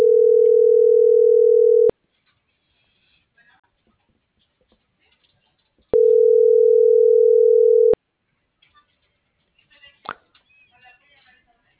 Ringback tone (North America), 440 + 480 Hz.
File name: CENTRAL BLN / out ximena.diaz - .9805